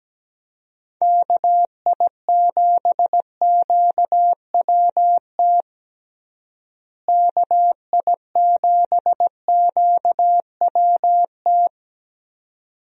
Morse code, heard as a keyed tone, about 17 wpm, 700 Hz. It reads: KI7QWT KI7QWT
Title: USA Call Signs - 6 Characters - Repeated 2x - 17wpm